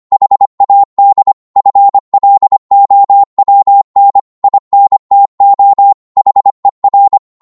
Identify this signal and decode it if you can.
Morse code, heard as a keyed tone, about 25 wpm, 800 Hz. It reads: HAD FLOWN INTO HER
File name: Morse / psg-25wpm